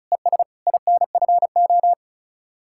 Morse code, heard as a keyed tone, about 44 wpm, 700 Hz. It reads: ES INFO